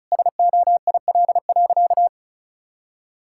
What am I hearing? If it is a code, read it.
Morse code, heard as a keyed tone, about 36 wpm, 700 Hz. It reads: SOIL.
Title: Sentences from Top 1000 Words - Review Part 5 - 35wpm